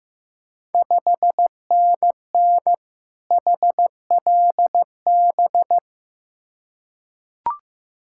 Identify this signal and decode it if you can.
Morse code, heard as a keyed tone, about 15 wpm, 690 Hz. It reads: 5NN HLB
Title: Ontario QSO Party Exchange 15wpm